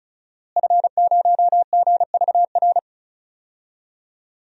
Morse code, heard as a keyed tone, about 36 wpm, 700 Hz. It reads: F0GVR
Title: DXCC-100 Call Signs - 5 Characters - Repeated 3x - 35wpm